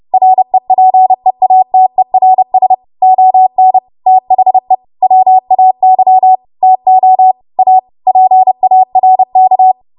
Morse code, heard as a keyed tone, about 30 wpm, 750 Hz. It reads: REPEATERS ON THE WAY TO A PARK
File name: Morse / 30WPM